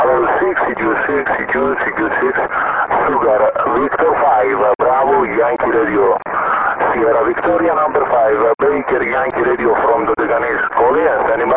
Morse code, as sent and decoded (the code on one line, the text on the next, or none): none